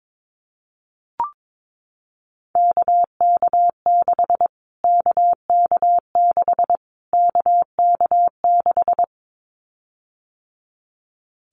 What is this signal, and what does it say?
Morse code, heard as a keyed tone, about 22 wpm, 700 Hz. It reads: KK6 KK6 KK6
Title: Callsign Prefixes for Top 25 DXCC - Repeated 3x - 22wpm